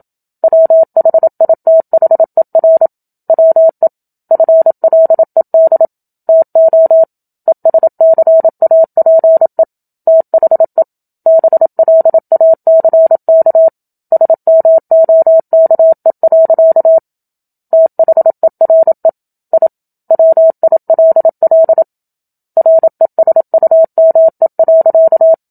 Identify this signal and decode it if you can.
Morse code, heard as a keyed tone, about 27 wpm, 650 Hz. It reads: WHITHER WE FLED TO ESCAPE THE BLACK SMOKE. THERE I WILL RESUME.
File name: Book Two / 01-Chapter 1 - 27 WPM